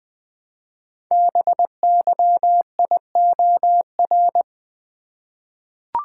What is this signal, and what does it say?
Morse code, heard as a keyed tone, about 20 wpm, 700 Hz. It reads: BYIOR
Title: Code Groups - 5 Letters - 20wpm